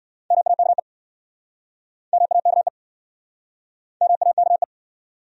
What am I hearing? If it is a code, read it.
Morse code, heard as a keyed tone, about 60 wpm, 690 Hz. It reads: MAKE MAKE MAKE
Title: Speed-Racing Top 100 Words 50wpm